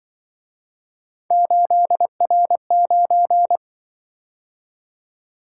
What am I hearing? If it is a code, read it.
Morse code, heard as a keyed tone, about 25 wpm, 700 Hz. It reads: 8R9
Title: Speed-Racing - Three Letter-Number Combinations - 20wpm